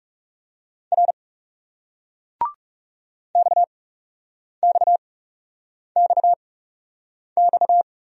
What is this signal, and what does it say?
Morse code, heard as a keyed tone, about 38 wpm, spaced as for 8 wpm, 700 Hz. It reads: R XXXX